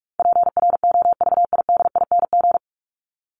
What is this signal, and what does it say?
Morse code, heard as a keyed tone, about 44 wpm, 700 Hz. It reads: PROVIDING
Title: ING Suffix - Top 100 Words with Base Words 45wpm